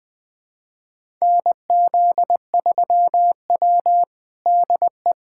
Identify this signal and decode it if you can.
Morse code, heard as a keyed tone, about 20 wpm, 700 Hz. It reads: NZ3W DE
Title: USA Callsign DE Callsign - 4 Characters - 20wpm